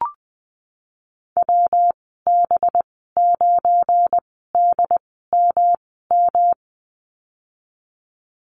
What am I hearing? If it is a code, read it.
Morse code, heard as a keyed tone, about 20 wpm, spaced as for 15 wpm, 700 Hz. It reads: WB9DMM
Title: 231 Lesson 37g - Letter x - Call Signs 20wpm